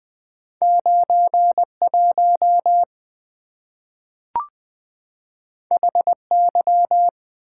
Morse code, heard as a keyed tone, about 20 wpm, 690 Hz. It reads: 91 HY
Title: Two Letter-Number Combinations - 20wpm